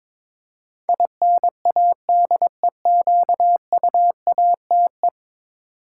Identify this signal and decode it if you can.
Morse code, heard as a keyed tone, about 22 wpm, 700 Hz. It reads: INADEQUATE
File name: practice / 22wpm / 10 Letter Words - First 3 Letters Given - 22wpm